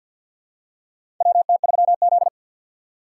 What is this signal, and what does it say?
Morse code, heard as a keyed tone, about 47 wpm, 700 Hz. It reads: WT3Z
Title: USA Call Signs - 6x Character Spacing - 50wpm